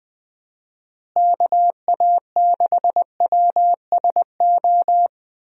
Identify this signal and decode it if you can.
Morse code, heard as a keyed tone, about 20 wpm, 700 Hz. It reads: KA6WSO
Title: USA Call Signs - 6 Characters - Repeated 2x - 20wpm